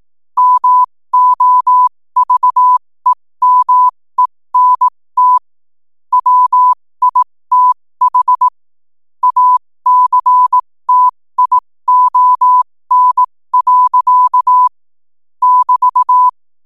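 Morse code, heard as a keyed tone, about 18 wpm, spaced as for 15 wpm, 1000 Hz. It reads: MOVEMENT WITH ACTION. <BT>
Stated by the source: 15 wpm Morse code from RNZN Communicators Association.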